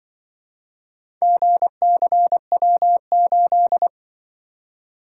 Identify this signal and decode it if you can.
Morse code, heard as a keyed tone, about 24 wpm, 700 Hz. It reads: GCW8
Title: Speed-Racing - Four Letter-Number Combinations - 20wpm